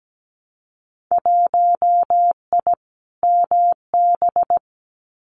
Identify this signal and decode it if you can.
Morse code, heard as a keyed tone, about 17 wpm, 700 Hz. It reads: 1I MB
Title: Winter Field Day Exchanges Encore 17wpm